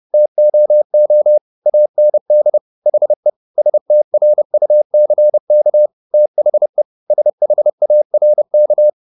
Morse code, heard as a keyed tone, about 30 wpm, 600 Hz. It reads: TOO AND HE STRUCK THE SHARK